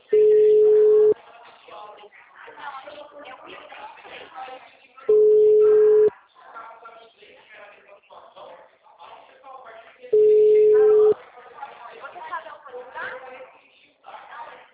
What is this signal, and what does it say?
Ringback tone (Europe, 425 Hz), with speech.
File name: out- -2011- - - .16848